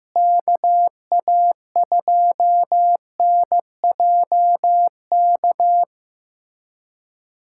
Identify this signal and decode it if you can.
Morse code, heard as a keyed tone, about 15 wpm, 700 Hz. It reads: KA2NJK
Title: USA Callsign DE Callsign - 6 Characters - 15wpm